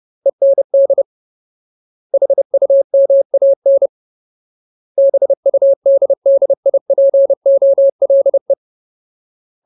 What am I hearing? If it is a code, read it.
Morse code, heard as a keyed tone, about 30 wpm, 550 Hz. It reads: END HUMAN BUDDIPOLE
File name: Ultimate Word List / 30WPM Ultimate